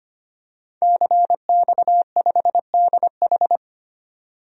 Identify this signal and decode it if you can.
Morse code, heard as a keyed tone, about 25 wpm, 700 Hz. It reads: CX5DH
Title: DXCC-100 Call Signs - 5 Characters - 25wpm